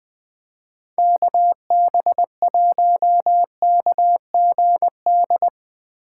Morse code, heard as a keyed tone, about 21 wpm, 700 Hz. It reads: KB1KGD
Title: Speed Racing - USA 2x3 Call Signs 17wpm